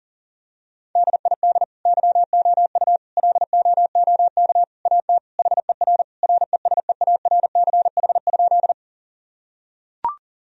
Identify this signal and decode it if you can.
Morse code, heard as a keyed tone, about 40 wpm, 700 Hz. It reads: DID YOU LOOK AT HER RESEARCH?
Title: Sentences from Top 1000 Words - Review Part 4 - 40wpm